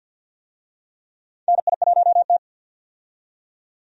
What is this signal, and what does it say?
Morse code, heard as a keyed tone, about 48 wpm, 700 Hz. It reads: NI1T
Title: USA Call Signs - 4 Characters - 50wpm